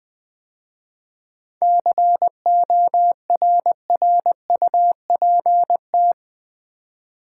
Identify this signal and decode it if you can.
Morse code, heard as a keyed tone, about 20 wpm, 700 Hz. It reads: CORRUPT
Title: 7 Letter Words - First 3 Letters Given - 20wpm